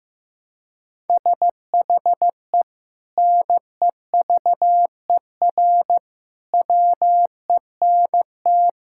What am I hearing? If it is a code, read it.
Morse code, heard as a keyed tone, about 15 wpm, 700 Hz. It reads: SHE NEVER WENT
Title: Sentences from Top 1000 Words - Review Part 5 - 15wpm